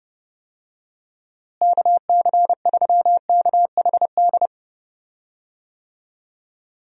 Morse code, heard as a keyed tone, about 30 wpm, 700 Hz. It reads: KC3KHD
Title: USA Call Signs - 6 Characters - Repeated 2x - 30wpm